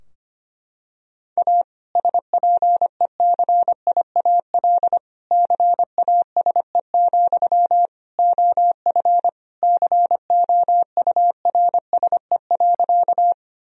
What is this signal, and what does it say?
Morse code, heard as a keyed tone, about 25 wpm, 700 Hz. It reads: A SPECIAL CASE, OF COURSE.
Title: Sentences from Top 1000 Words - Review Part 3 - 25wpm